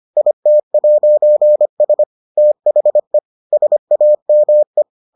Morse code, heard as a keyed tone, about 25 wpm, 600 Hz. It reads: IT'S THE SAME